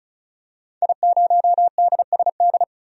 Morse code, heard as a keyed tone, about 36 wpm, 700 Hz. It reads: I0DSD
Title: DXCC-100 Call Signs - 5 Characters - 35wpm